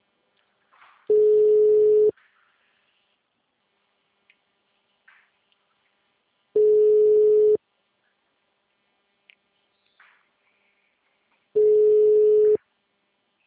Ringback tone (Europe, 425 Hz).
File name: out- -2010- - - .251